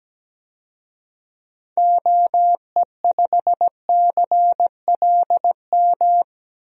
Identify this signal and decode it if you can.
Morse code, heard as a keyed tone, about 17 wpm, 700 Hz. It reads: OE5CLM
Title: DXCC-100 Call Signs - 6 Characters - 17wpm